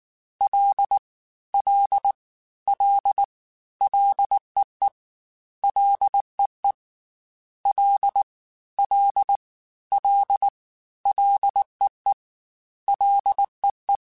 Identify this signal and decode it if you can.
Morse code, heard as a keyed tone, about 19 wpm, 800 Hz. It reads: L L L LEE LEE L L L LEE LEE
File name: Ls / Single L m3s / Single L Prefixe 3x2 Twice @18